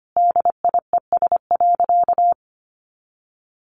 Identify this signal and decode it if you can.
Morse code, heard as a keyed tone, about 25 wpm, 700 Hz. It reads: DIES.
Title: A Sound Of Thunder 25wpm